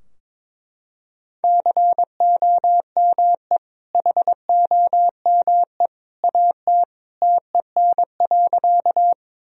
Morse code, heard as a keyed tone, about 22 wpm, 700 Hz. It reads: COME HOME AT TEN.
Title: Sentences from Top 1000 Words - Review Part 2 - 22wpm